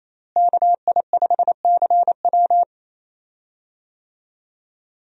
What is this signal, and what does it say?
Morse code, heard as a keyed tone, about 28 wpm, 700 Hz. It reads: KI5CW
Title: USA Call Signs - 5 Characters - Repeated 2x - 28wpm